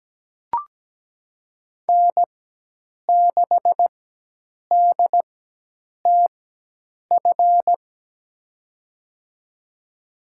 Morse code, heard as a keyed tone, about 17 wpm, spaced as for 8 wpm, 700 Hz. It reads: N6DTF
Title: USA Call Signs - 5 Characters - 4x Character Spacing - 17wpm